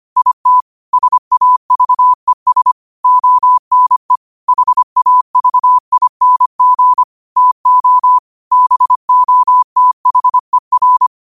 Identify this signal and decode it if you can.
Morse code, heard as a keyed tone, about 25 wpm, 1000 Hz. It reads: IT SAVES ONE HAVING TO BOTHER